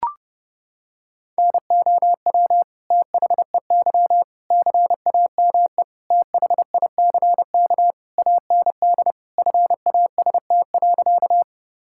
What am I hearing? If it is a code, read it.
Morse code, heard as a keyed tone, about 31 wpm, 700 Hz. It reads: NOW THEY CAME THICK AND FAST.
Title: Sentences from Top 1600 Words 30wpm